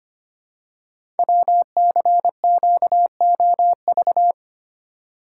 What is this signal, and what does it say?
Morse code, heard as a keyed tone, about 25 wpm, 700 Hz. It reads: WCQOV
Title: Code Groups - 5 Letters - 25wpm